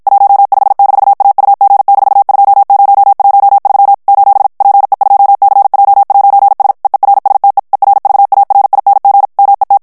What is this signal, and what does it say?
Morse code, heard as a keyed tone, about 52 wpm, 800 Hz. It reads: 15-AUG-2013 : PE2CJ'S EERSTE ERVARING MET
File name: morse mp3 / HOLLANDAIS texte clair 2 vitesse 50 mots minute